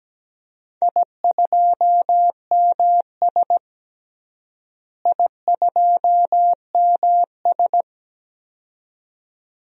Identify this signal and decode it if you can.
Morse code, heard as a keyed tone, about 17 wpm, 700 Hz. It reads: I2MS I2MS
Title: DXCC-100 Call Signs - 4 Characters - Repeated 2x - 17wpm